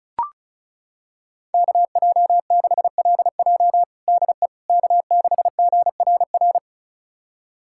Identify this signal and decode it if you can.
Morse code, heard as a keyed tone, about 34 wpm, 700 Hz. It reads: KJ6LJ DE K6GRR
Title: USA Callsign DE Callsign - 5 Characters - 35wpm